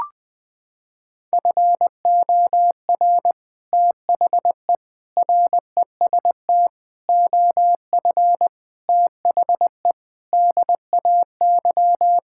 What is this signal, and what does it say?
Morse code, heard as a keyed tone, about 21 wpm, 700 Hz. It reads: FOR THE REST OF THE DAY
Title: Sentences with 3-Word Phrases - 20wpm